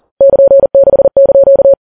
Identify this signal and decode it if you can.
Morse code, heard as a keyed tone, about 40 wpm, 550 Hz. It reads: <KN>6)
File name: Files Web tls / Morse on9fv / morse40kps